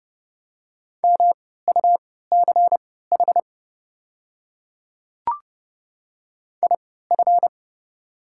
Morse code, heard as a keyed tone, about 30 wpm, spaced as for 17 wpm, 700 Hz. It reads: MUCH IF
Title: 3x Character Spacing - Top 200 Words - 30wpm